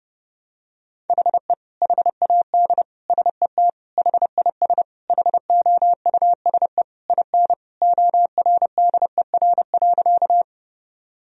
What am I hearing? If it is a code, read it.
Morse code, heard as a keyed tone, about 32 wpm, 700 Hz. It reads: HE HAD SET HIS HOUSE IN ORDER.
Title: Sentences from Top 1000 Words - Review Part 2 - 30wpm